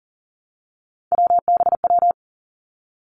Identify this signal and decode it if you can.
Morse code, heard as a keyed tone, about 38 wpm, 700 Hz. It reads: WBW
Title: Three Challenging Letter-Number Combinations 40wpm